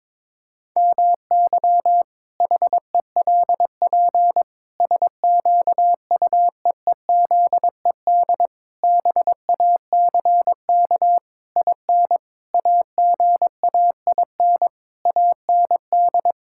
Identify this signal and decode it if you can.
Morse code, heard as a keyed tone, about 22 wpm, 700 Hz. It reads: MY HELP SQUEEZED BACK IN AGAIN AND
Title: Pangram Challenge v2 - 22wpm